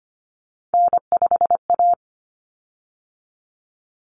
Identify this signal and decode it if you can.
Morse code, heard as a keyed tone, about 25 wpm, 700 Hz. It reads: N5A
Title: Common Cut Numbers - 3-Digit 25wpm